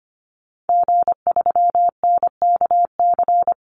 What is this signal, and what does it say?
Morse code, heard as a keyed tone, about 25 wpm, 700 Hz. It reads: G3NKC
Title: Call Signs Repeated 2x - 25wpm